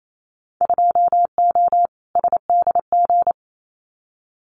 Morse code, heard as a keyed tone, about 28 wpm, 700 Hz. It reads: 2O SDG
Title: Winter Field Day Exchanges 28wpm